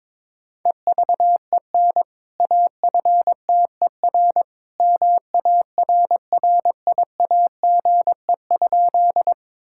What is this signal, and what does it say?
Morse code, heard as a keyed tone, about 22 wpm, 700 Hz. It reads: EVEN AFTER MARRIAGE?
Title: Sentences from Top 1000 Words - Review Part 5 - 22wpm